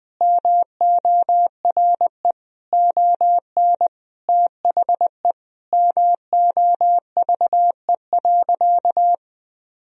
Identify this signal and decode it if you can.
Morse code, heard as a keyed tone, about 20 wpm, 700 Hz. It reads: MORE ON THE MOVE.
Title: Sentences from Top 1000 Words - Review Part 3 - 20wpm